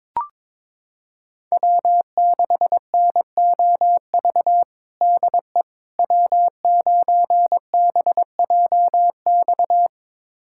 Morse code, heard as a keyed tone, about 22 wpm, 700 Hz. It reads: W6NOV DE W9BJX
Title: USA Callsign DE Callsign - 5 Characters - 22wpm